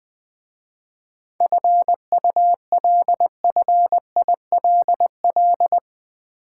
Morse code, heard as a keyed tone, about 20 wpm, 700 Hz. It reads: FULFILL
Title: 7 Letter Words - First 3 Letters Given - 20wpm